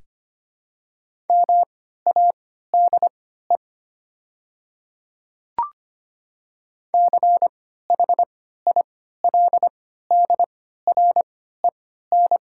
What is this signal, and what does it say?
Morse code, heard as a keyed tone, about 25 wpm, spaced as for 14 wpm, 700 Hz. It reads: MADE CHILDREN
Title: 3x Character Spacing - Top 500 Words - 25wpm